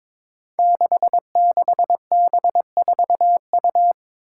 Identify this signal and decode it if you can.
Morse code, heard as a keyed tone, about 22 wpm, 700 Hz. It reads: 66B4U
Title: Five Challenging Letter-Number Combinations 22wpm